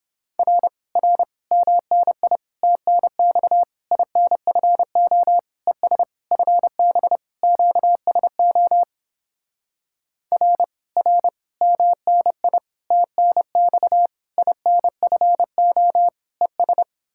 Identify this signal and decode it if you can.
Morse code, heard as a keyed tone, about 29 wpm, 700 Hz. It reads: R R MNI TNX INFO ES FB QSO R R MNI TNX INFO ES